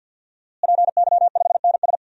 Morse code, heard as a keyed tone, about 53 wpm, 700 Hz. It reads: PY5NS
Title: DXCC-100 Call Signs - 5 Characters - Repeated 3x - 50wpm